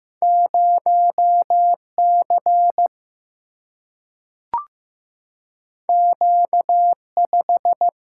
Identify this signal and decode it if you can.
Morse code, heard as a keyed tone, about 15 wpm, 700 Hz. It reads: 0 C Q 5
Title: Single-Letter Single-Number 15wpm